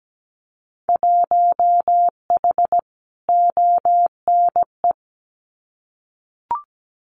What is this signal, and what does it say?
Morse code, heard as a keyed tone, about 17 wpm, 700 Hz. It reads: 1H ONE
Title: Winter Field Day Exchanges 17wpm